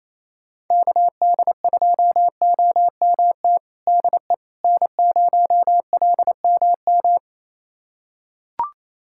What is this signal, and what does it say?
Morse code, heard as a keyed tone, about 27 wpm, 700 Hz. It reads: KD2OMT DE N0LMM
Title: USA Callsign DE Callsign 28wpm